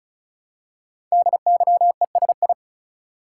Morse code, heard as a keyed tone, about 36 wpm, 690 Hz. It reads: DYESI
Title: Code Groups - 5 Letters - 35wpm